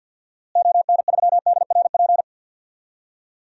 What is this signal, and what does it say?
Morse code, heard as a keyed tone, about 50 wpm, 700 Hz. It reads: ON3DRP
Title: DXCC-100 Call Signs - 6 Characters - Repeated 2x - 50wpm